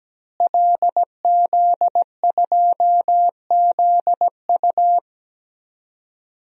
Morse code, heard as a keyed tone, about 17 wpm, 700 Hz. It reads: LZ2ZU
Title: DXCC-100 Call Signs - 5 Characters - Repeated 2x - 17wpm